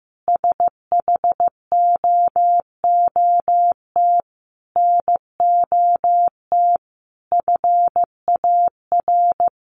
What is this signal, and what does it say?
Morse code, heard as a keyed tone, about 15 wpm, 700 Hz. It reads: SHOOT NOT FAR
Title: Sentences from Top 1200 Words 15wpm